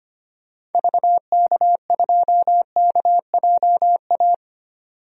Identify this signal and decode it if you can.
Morse code, heard as a keyed tone, about 25 wpm, 700 Hz. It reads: VK2KJA
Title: DXCC-100 Call Signs - 6 Characters - 25wpm